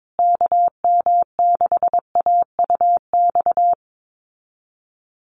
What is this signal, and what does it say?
Morse code, heard as a keyed tone, about 22 wpm, 700 Hz. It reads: KM6AUX
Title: USA Call Signs Repeated 2x 22wpm